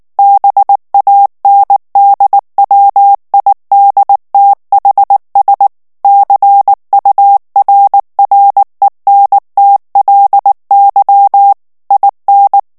Morse code, heard as a keyed tone, about 19 wpm, 800 Hz. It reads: BANDWIDTHS CURRENTLY IN
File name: ANGLAIS texte clair 3 vitesse 18 mots minute